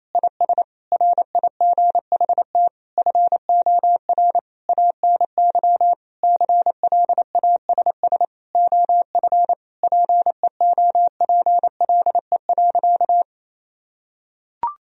Morse code, heard as a keyed tone, about 28 wpm, 700 Hz. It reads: IS RIGHT FOR ANY CLASS OF PEOPLE.
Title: Sentences from Top 1000 Words - Review Part 2 - 28wpm